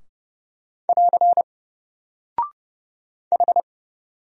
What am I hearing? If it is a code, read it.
Morse code, heard as a keyed tone, about 31 wpm, 700 Hz. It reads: <AR> H
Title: Single Character (Letter:Number:Punctuation:Prosign) 30wpm